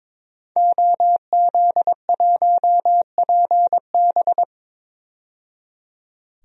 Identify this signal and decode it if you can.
Morse code, heard as a keyed tone, about 22 wpm, 700 Hz. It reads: OZ1PB
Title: DXCC-100 Call Signs - 5 Characters - 22wpm